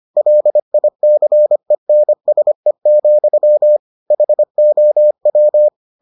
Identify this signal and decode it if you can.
Morse code, heard as a keyed tone, about 25 wpm, 600 Hz. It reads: LICENSE, HOW